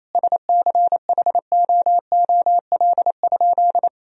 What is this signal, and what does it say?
Morse code, heard as a keyed tone, about 28 wpm, 700 Hz. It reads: SCHOOL?
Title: Sentences from Top 1000 Words - Review Part 1 - 28wpm